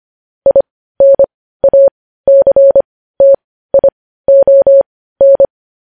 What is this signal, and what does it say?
Morse code, heard as a keyed tone, about 25 wpm, spaced as for 15 wpm, 550 Hz. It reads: INACTION
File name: podcast / News headlines-1x-25x15